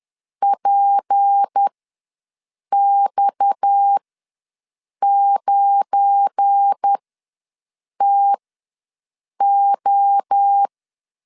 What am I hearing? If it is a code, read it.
Morse code, heard as a keyed tone, about 10 wpm, spaced as for 6 wpm, 800 Hz. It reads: PX9TO
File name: LEZIONI / 7 GF9Z / 7 GF9Z ws 10wpm 3919